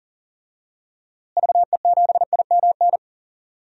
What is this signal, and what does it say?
Morse code, heard as a keyed tone, about 37 wpm, 700 Hz. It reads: VE7IMN
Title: DXCC-100 Call Signs - 6 Characters - Repeated 2x - 40wpm